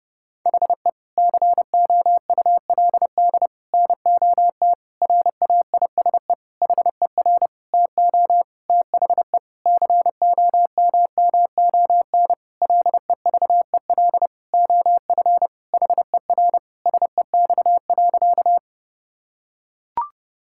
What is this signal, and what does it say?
Morse code, heard as a keyed tone, about 30 wpm, 700 Hz. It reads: HE COULD NOT RAISE HER TO THE COMMON LEVEL OF HER SEX.
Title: Sentences from Top 1600 Words 30wpm